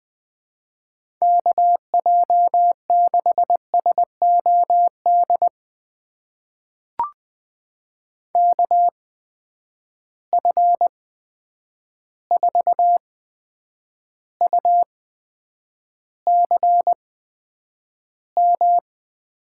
Morse code, heard as a keyed tone, about 20 wpm, spaced as for 9 wpm, 700 Hz. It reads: KJ6SOD KF4UCM
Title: USA Call Signs - 6 Characters - 8x Character Spacing - 20wpm